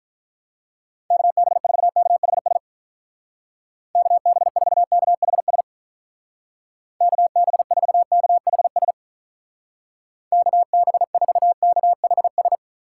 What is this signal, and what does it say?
Morse code, heard as a keyed tone, about 46 wpm, 700 Hz. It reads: KB4KHS KB4KHS KB4KHS KB4KHS